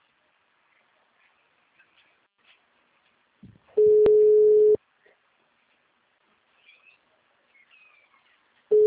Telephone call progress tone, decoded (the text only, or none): ringback tone